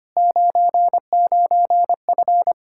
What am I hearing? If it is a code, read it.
Morse code, heard as a keyed tone, about 25 wpm, 700 Hz. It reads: 99F
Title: Practice QSO - Exchange 2 - Info - 25wpm